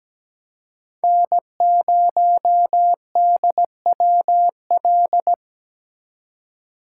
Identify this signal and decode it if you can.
Morse code, heard as a keyed tone, about 20 wpm, 700 Hz. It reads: N0DWL
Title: Speed Racing - USA 1x3 Call Signs 17wpm